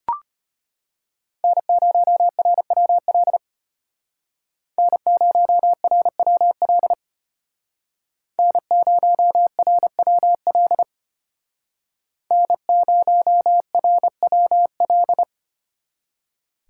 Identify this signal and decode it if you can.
Morse code, heard as a keyed tone, about 32 wpm, 700 Hz. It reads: N0RWL N0RWL N0RWL N0RWL